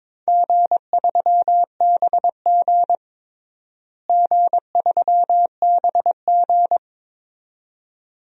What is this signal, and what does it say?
Morse code, heard as a keyed tone, about 22 wpm, 700 Hz. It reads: G3BG G3BG
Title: DXCC-100 Call Signs - 4 Characters - Repeated 2x - 22wpm